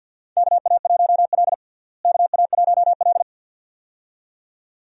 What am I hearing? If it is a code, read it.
Morse code, heard as a keyed tone, about 52 wpm, 700 Hz. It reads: KA1L KA1L
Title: USA Call Signs - 4 Characters - Repeated 2x - 50wpm